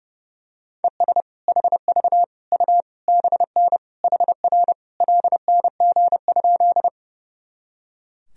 Morse code, heard as a keyed tone, about 30 wpm, 700 Hz. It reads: ES HV U BN HR LNG?
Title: Sentences from Top 200 Words and Abbreviations 30wpm